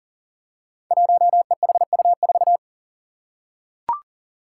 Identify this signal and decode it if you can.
Morse code, heard as a keyed tone, about 40 wpm, 700 Hz. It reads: 1EHU4
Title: Five Challenging Letter-Number Combinations 40wpm